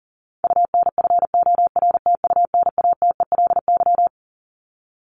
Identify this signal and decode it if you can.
Morse code, heard as a keyed tone, about 40 wpm, 700 Hz. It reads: UNFORTUNATELY
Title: Top 1751-2000 Words 40wpm